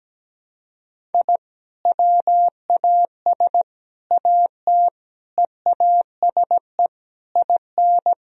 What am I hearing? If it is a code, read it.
Morse code, heard as a keyed tone, about 17 wpm, 700 Hz. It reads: I WAS AT EASE IN
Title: Sentences from Top 1000 Words - Review Part 1 - 17wpm